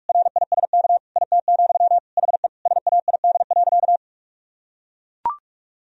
Morse code, heard as a keyed tone, about 41 wpm, 700 Hz. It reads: RISK IT, HE SAID.
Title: Sentences from Top 1000 Words - Review Part 4 - 45wpm